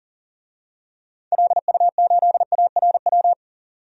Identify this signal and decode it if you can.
Morse code, heard as a keyed tone, about 40 wpm, 700 Hz. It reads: LU8ARW
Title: DXCC-100 Call Signs - 6 Characters - 40wpm